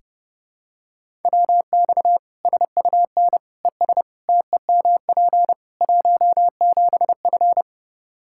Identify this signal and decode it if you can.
Morse code, heard as a keyed tone, about 30 wpm, 700 Hz. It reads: WX SUN ES TEMP 17F
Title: Practice QSO - Exchange 2 - Info - 30wpm